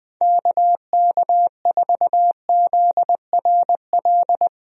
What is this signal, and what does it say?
Morse code, heard as a keyed tone, about 20 wpm, 700 Hz. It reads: KK4ZRL
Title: USA Call Signs - 6 Characters - Repeated 2x - 20wpm